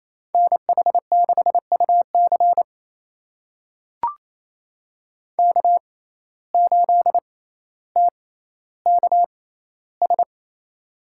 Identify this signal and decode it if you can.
Morse code, heard as a keyed tone, about 28 wpm, spaced as for 14 wpm, 700 Hz. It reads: NH6UC K8TKS
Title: USA Call Signs - 6x Character Spacing - 28wpm